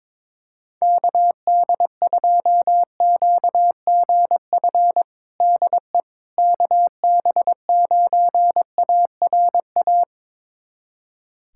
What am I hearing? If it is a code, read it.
Morse code, heard as a keyed tone, about 22 wpm, 700 Hz. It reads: KD2QGF DE KB9ARA
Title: USA Callsign DE Callsign - 6 Characters - 22wpm